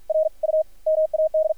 Morse code, heard as a keyed tone, about 53 wpm, 590 Hz. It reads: R U MAD
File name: archived / CRSnet